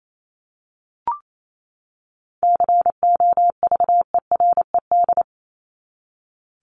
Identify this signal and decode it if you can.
Morse code, heard as a keyed tone, about 28 wpm, 700 Hz. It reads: COVERED
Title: ED Suffix - Top 100 Words with Base Words 28wpm